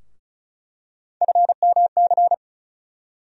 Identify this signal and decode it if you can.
Morse code, heard as a keyed tone, about 36 wpm, 700 Hz. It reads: FMC
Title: USA Call Sign Postfixes - 3 Letters 35wpm